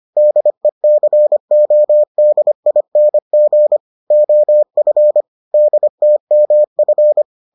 Morse code, heard as a keyed tone, about 25 wpm, 600 Hz. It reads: DECODING OF DTMF